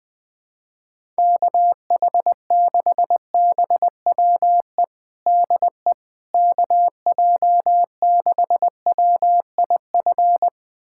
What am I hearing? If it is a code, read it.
Morse code, heard as a keyed tone, about 20 wpm, 700 Hz. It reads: KH6BWE DE KJ6WIF
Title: USA Callsign DE Callsign 20wpm